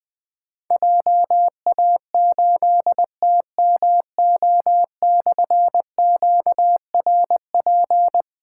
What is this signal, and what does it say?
Morse code, heard as a keyed tone, about 20 wpm, 700 Hz. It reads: JA8TMO/QRP
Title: Difficult Call Signs 20wpm